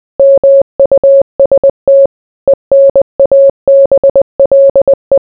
Morse code, heard as a keyed tone, about 20 wpm, 550 Hz. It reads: MUST ENABLE